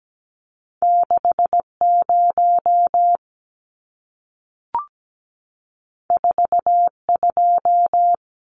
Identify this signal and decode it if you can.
Morse code, heard as a keyed tone, about 17 wpm, 700 Hz. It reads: 60 42
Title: Numbers - 2-Digit 17wpm